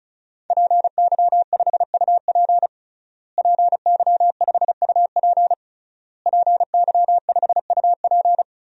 Morse code, heard as a keyed tone, about 36 wpm, 700 Hz. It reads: PY5UP PY5UP PY5UP
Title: DXCC-100 Call Signs - 5 Characters - Repeated 3x - 35wpm